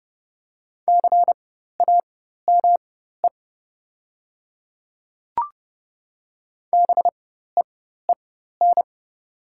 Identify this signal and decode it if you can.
Morse code, heard as a keyed tone, about 31 wpm, spaced as for 14 wpm, 700 Hz. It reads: CAME BEEN
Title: 4x Character Spacing - Top 200 Words - 30wpm